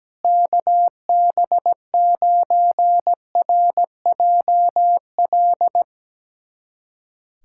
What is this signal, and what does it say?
Morse code, heard as a keyed tone, about 17 wpm, 700 Hz. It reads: KB9RJL
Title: USA Callsign DE Callsign - 6 Characters - 17wpm